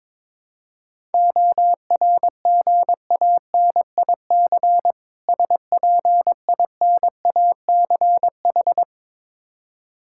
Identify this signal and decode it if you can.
Morse code, heard as a keyed tone, about 22 wpm, 690 Hz. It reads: ORGANIC SPINACH
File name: Difficult Words with Hints - 22wpm